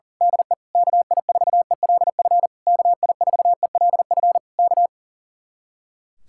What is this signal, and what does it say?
Morse code, heard as a keyed tone, about 35 wpm, 700 Hz. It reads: DE KI4ELF KI4ELF K
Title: Speed-Racing Common QSO Phrases Encore - 40wpm